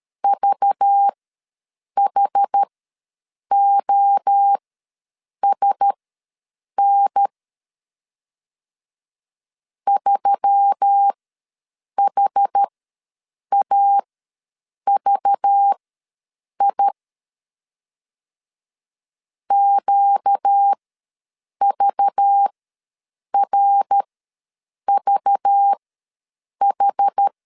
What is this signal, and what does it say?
Morse code, heard as a keyed tone, about 13 wpm, spaced as for 7 wpm, 800 Hz. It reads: VHOSN 3HAVI QVRVH
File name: LEZIONI / 5 LBVC / 5 LBV ws 12wpm 3015